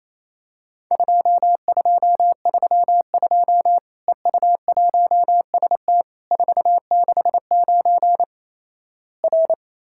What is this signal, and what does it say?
Morse code, heard as a keyed tone, about 28 wpm, 700 Hz. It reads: 2232 EU1ST 469 R
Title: Worked All Europe DX Contest - QTC Exchange - 28wpm